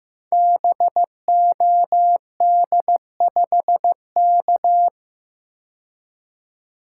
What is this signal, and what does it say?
Morse code, heard as a keyed tone, about 15 wpm, 700 Hz. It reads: BOD5K
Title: Code Groups - 5 Letter:Numbers - 15wpm